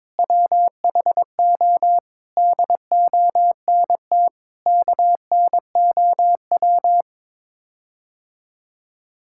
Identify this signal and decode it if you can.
Morse code, heard as a keyed tone, about 22 wpm, 700 Hz. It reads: WHO DONT KNOW
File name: practice / 22wpm / Sets of 7 Words - 22wpm